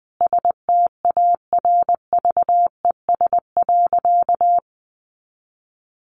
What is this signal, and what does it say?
Morse code, heard as a keyed tone, about 20 wpm, 700 Hz. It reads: STARVES.
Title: A Sound Of Thunder 20wpm